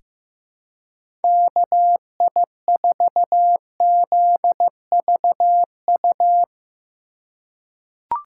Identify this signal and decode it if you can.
Morse code, heard as a keyed tone, about 15 wpm, 700 Hz. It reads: KI4ZVU
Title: K1USN SST QSO Practice - 15wpm